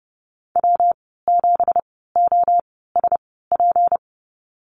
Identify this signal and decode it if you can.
Morse code, heard as a keyed tone, about 28 wpm, spaced as for 17 wpm, 700 Hz. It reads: W7OSP
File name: Lesson 37g - Letter x - Call Signs 30wpm